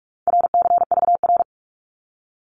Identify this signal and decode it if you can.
Morse code, heard as a keyed tone, about 46 wpm, 700 Hz. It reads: RCVR
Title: Abbreviations 45wpm